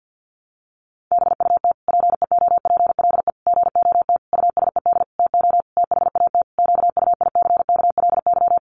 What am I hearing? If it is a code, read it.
Morse code, heard as a keyed tone, about 50 wpm, 700 Hz. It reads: BUT PEOPLE GOT USED TO THAT QUICKLY
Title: Sentences from Top 700 Words 50wpm